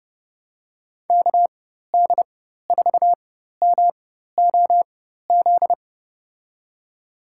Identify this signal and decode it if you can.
Morse code, heard as a keyed tone, about 31 wpm, spaced as for 14 wpm, 700 Hz. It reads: KD4MOZ
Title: USA Call Signs - 6 Characters - 4x Character Spacing - 30wpm